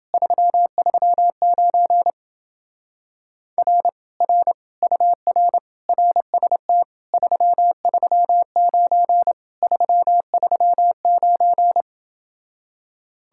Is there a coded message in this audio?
Yes: Morse code from 0.1 to 11.8 s, heard as a keyed tone, about 30 wpm, 700 Hz. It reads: 339 R R UR RST 339 339